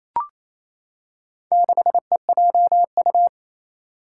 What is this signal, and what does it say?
Morse code, heard as a keyed tone, about 28 wpm, 700 Hz. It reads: 6EJU
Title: Four Challenging Letter-Number Combinations 28wpm